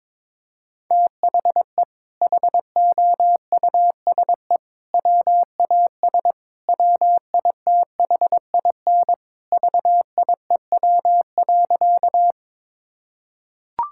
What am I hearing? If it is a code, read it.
Morse code, heard as a keyed tone, about 22 wpm, 700 Hz. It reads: THE HOUSE WAS WITHIN VIEW.
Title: Sentences from Top 1000 Words - Review Part 6 - 22wpm